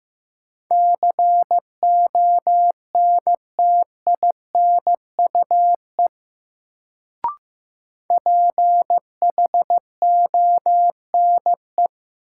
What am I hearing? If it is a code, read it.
Morse code, heard as a keyed tone, about 16 wpm, 700 Hz. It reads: CONTINUE PHONE
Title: Sentences from Top 1100 Words - New Word Drill Alt - 15wpm